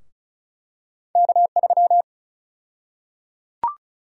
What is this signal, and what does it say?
Morse code, heard as a keyed tone, about 36 wpm, 700 Hz. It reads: K3
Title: USA Call Sign Prefixes - 35wpm